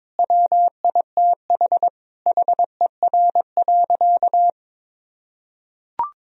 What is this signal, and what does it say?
Morse code, heard as a keyed tone, about 22 wpm, 700 Hz. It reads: WITH HER.
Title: Sentences from Top 1000 Words - Review Part 4 - 22wpm